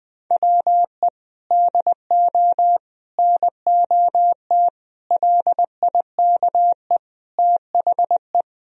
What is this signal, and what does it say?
Morse code, heard as a keyed tone, about 20 wpm, 700 Hz. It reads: WE DO NOT LIKE THE
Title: Sentences from Top 1000 Words - Review Part 4 - 20wpm